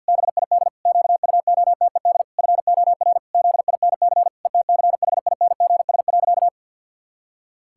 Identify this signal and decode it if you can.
Morse code, heard as a keyed tone, about 50 wpm, 700 Hz. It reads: BID QUOTED FOR ZINC ETCHINGS.
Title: Pangram Challenge v2 - 50wpm